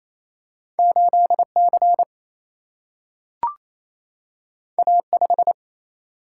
Morse code, heard as a keyed tone, about 28 wpm, 700 Hz. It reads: 8C A5
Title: Single-Letter Single-Number 28wpm